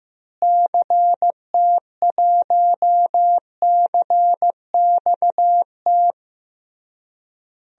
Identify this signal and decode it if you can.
Morse code, heard as a keyed tone, about 16 wpm, 700 Hz. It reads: CT1CXT
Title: DXCC-100 Call Signs - 6 Characters - Repeated 3x - 15wpm